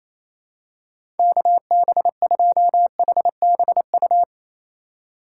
Morse code, heard as a keyed tone, about 28 wpm, 700 Hz. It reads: KB2HBU
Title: USA Call Signs - 6 Characters - 3x Character Spacing - 28wpm